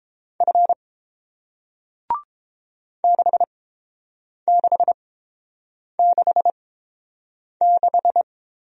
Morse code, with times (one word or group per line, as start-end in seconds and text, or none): 0.4-0.7 F
3.0-8.2 6666